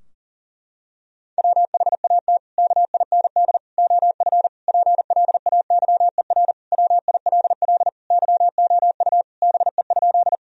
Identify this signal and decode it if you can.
Morse code, heard as a keyed tone, about 40 wpm, 700 Hz. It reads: WHAT KIND OF PLAYER WILL YOU BE?
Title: Sentences from Top 1000 Words - Review Part 4 - 40wpm